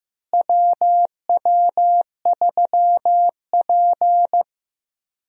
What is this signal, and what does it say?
Morse code, heard as a keyed tone, about 15 wpm, 700 Hz. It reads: WW3P
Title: USA Call Signs - 4 Characters - Repeated 3x - 15wpm